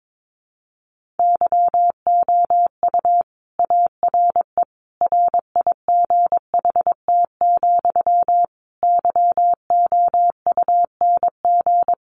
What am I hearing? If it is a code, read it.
Morse code, heard as a keyed tone, about 22 wpm, 700 Hz. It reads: YOU ARE RIGHT, YOUNG
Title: Sentences from Top 400 Words 22wpm